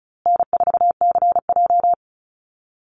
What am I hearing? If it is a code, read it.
Morse code, heard as a keyed tone, about 34 wpm, 700 Hz. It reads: N4CJ
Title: Easy Call Signs Repeated 3x - 35wpm